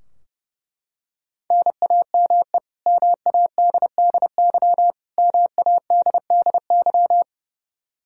Morse code, heard as a keyed tone, about 30 wpm, 700 Hz. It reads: NAME MADDY MADDY
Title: Practice QSO - Exchange 1 - Beginning - 30wpm